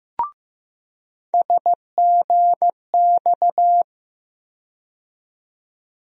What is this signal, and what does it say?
Morse code, heard as a keyed tone, about 15 wpm, 700 Hz. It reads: SGX
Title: USA Call Sign Postfixes - 3 Letters 15wpm